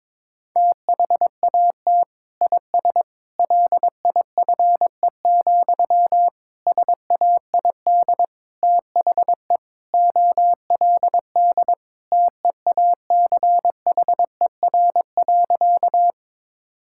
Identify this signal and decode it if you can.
Morse code, heard as a keyed tone, about 22 wpm, 700 Hz. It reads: THAT IS LIFE, SAID THE OLD TEACHER.
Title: Sentences from Top 1000 Words - Review Part 6 - 22wpm